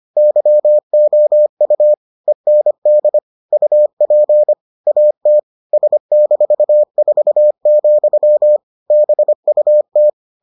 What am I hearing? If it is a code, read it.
Morse code, heard as a keyed tone, about 25 wpm, 600 Hz. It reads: YOU END UP AT S-4, BUT